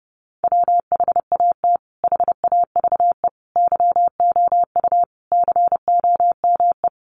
Morse code, heard as a keyed tone, about 28 wpm, 700 Hz. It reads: WHAT HAVE YOU COME
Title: Sentences from Top 100 Words 30wpm